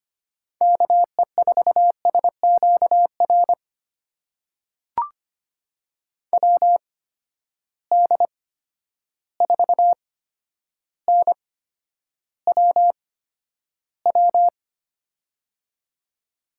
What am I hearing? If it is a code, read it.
Morse code, heard as a keyed tone, about 25 wpm, spaced as for 11 wpm, 700 Hz. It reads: KE4SQR WD4NWW
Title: USA Call Signs - 6 Characters - 8x Character Spacing - 25wpm